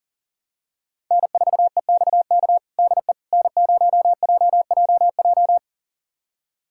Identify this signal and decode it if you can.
Morse code, heard as a keyed tone, about 40 wpm, 700 Hz. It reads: N4EXK DE N0JJJ
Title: USA Callsign DE Callsign - 5 Characters - 40wpm